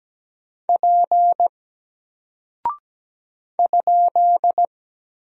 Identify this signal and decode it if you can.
Morse code, heard as a keyed tone, about 17 wpm, 700 Hz. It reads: P ?
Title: Single Character (Letter:Number:Punctuation:Prosign) Alt 17wpm